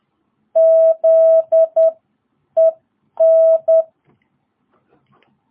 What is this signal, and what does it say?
Morse code, heard as a keyed tone, about 10 wpm, spaced as for 8 wpm, 660 Hz. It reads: ZEN
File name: jh8cbh / 2022cw